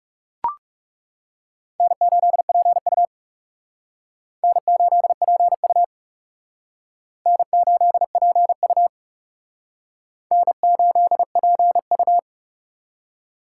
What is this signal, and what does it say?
Morse code, heard as a keyed tone, about 40 wpm, 700 Hz. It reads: N8PU N8PU N8PU N8PU